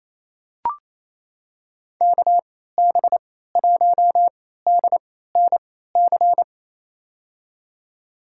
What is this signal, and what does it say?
Morse code, heard as a keyed tone, about 28 wpm, spaced as for 16 wpm, 700 Hz. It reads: KB1DNC
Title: USA Call Signs - 6 Characters - 3x Character Spacing - 28wpm